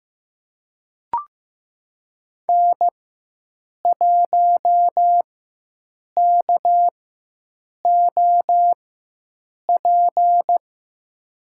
Morse code, heard as a keyed tone, about 15 wpm, spaced as for 7 wpm, 700 Hz. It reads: N1KOP
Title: USA Call Signs - 4x Character Spacing - 15wpm